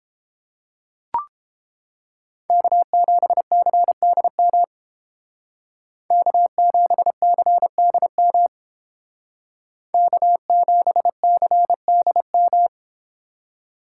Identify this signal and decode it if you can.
Morse code, heard as a keyed tone, about 30 wpm, 700 Hz. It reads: K7CDM K7CDM K7CDM